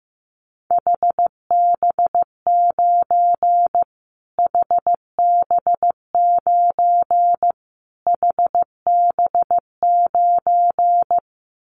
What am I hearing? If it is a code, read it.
Morse code, heard as a keyed tone, about 15 wpm, 700 Hz. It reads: HB9 HB9 HB9
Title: Callsign Prefixes for Top 25 DXCC - Repeated 3x - 15wpm